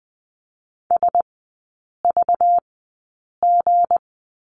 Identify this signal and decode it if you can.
Morse code, heard as a keyed tone, about 20 wpm, spaced as for 8 wpm, 700 Hz. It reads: SVG
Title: 212 Lesson 34e - Number 8 - Three Characters 20wpm